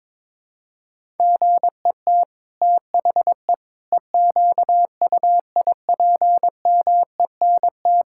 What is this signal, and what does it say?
Morse code, heard as a keyed tone, about 22 wpm, 700 Hz. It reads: GET THE EQUIPMENT
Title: Sentences from Top 1000 Words - Review Part 5 - 22wpm